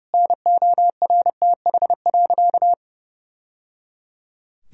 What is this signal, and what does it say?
Morse code, heard as a keyed tone, about 30 wpm, 700 Hz. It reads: NORTH.
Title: Sentences from Top 1000 Words - Review Part 1 - 30wpm